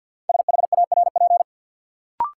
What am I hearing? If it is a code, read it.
Morse code, heard as a keyed tone, about 52 wpm, 700 Hz. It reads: SHARP
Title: Sentences from Top 1600 Words - New Word Drill - 50wpm